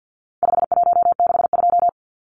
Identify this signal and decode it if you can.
Morse code, heard as a keyed tone, about 50 wpm, 700 Hz. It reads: 5162
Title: Numbers - 4-Digit 50wpm